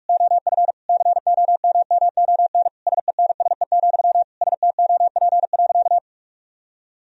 Morse code, heard as a keyed tone, about 46 wpm, 700 Hz. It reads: OF COMMON SENSE, STOP.
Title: Sentences from Top 1000 Words - Review Part 4 - 45wpm